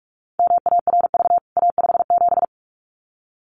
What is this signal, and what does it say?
Morse code, heard as a keyed tone, about 44 wpm, 700 Hz. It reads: MARV A57
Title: CWT Exchange 45wpm